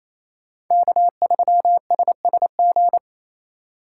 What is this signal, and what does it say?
Morse code, heard as a keyed tone, about 34 wpm, 700 Hz. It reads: K3SSG
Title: Speed Racing - USA 1x3 Call Signs 28wpm